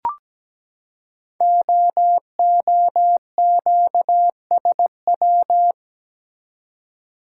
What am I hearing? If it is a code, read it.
Morse code, heard as a keyed tone, about 17 wpm, 700 Hz. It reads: OOQSW
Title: Code Groups - 5 Letters - 17wpm